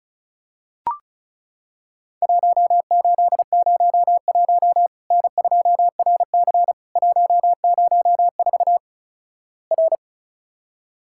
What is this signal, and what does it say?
Morse code, heard as a keyed tone, about 36 wpm, 700 Hz. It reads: 1801 N2RC 104 R
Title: Worked All Europe DX Contest - QTC Exchange - 35wpm